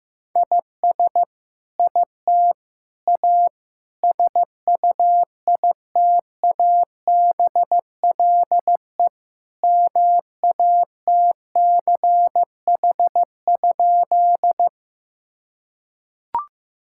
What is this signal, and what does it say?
Morse code, heard as a keyed tone, about 15 wpm, 700 Hz. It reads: IS IT A SUITABLE MATCH?
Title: Sentences from Top 1600 Words 15wpm